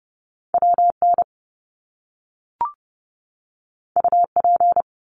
Morse code, heard as a keyed tone, about 30 wpm, 700 Hz. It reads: WN UP
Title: Top Two Letter Combinations in Words 30wpm